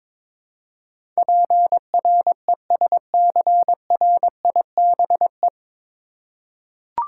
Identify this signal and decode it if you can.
Morse code, heard as a keyed tone, about 22 wpm, 700 Hz. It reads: PRESCRIBE
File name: 9 Letter Words - First 3 Letters Given - 22wpm